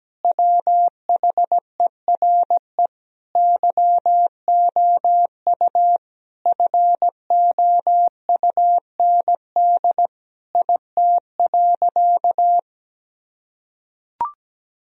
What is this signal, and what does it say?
Morse code, heard as a keyed tone, about 17 wpm, 700 Hz. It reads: WHERE YOU FOUND IT.
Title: Sentences from Top 1000 Words - Review Part 4 - 17wpm